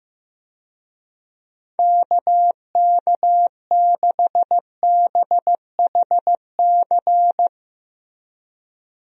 Morse code, heard as a keyed tone, about 15 wpm, 700 Hz. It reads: KK6BHC
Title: USA Call Signs - 6 Characters - Repeated 2x - 15wpm